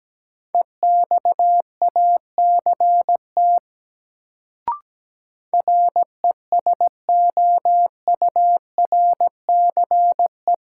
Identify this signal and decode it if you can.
Morse code, heard as a keyed tone, about 17 wpm, 700 Hz. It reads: EXACT RESOURCE
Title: Sentences from Top 1300 Words - New Word Drill - 17wpm